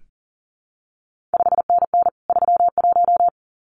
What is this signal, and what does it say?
Morse code, heard as a keyed tone, about 40 wpm, 700 Hz. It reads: 5NN 31
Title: CQ WW DX Contest Exchange Encore - 40wpm